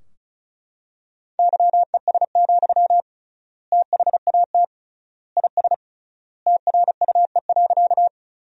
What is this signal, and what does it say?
Morse code, heard as a keyed tone, about 36 wpm, 700 Hz. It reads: YES, THAT IS TRUE.
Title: 3x Word-Spacing - Sentences from Top 500 Words - 35wpm